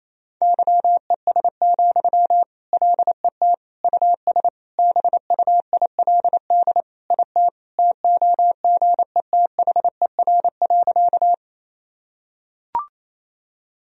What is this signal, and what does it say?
Morse code, heard as a keyed tone, about 28 wpm, 700 Hz. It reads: YES, LET US BUILD IT TOGETHER.
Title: Sentences from Top 1000 Words - Review Part 1 - 28wpm